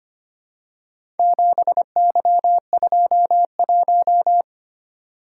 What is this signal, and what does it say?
Morse code, heard as a keyed tone, about 25 wpm, 700 Hz. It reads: 7Y21
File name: practice / 25wpm / Four Letter-Number Combinations - 25wpm